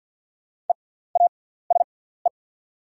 Morse code, heard as a keyed tone, about 48 wpm, spaced as for 17 wpm, 690 Hz. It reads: EASE
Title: 6x Character Spacing - Top 500 Words - 50wpm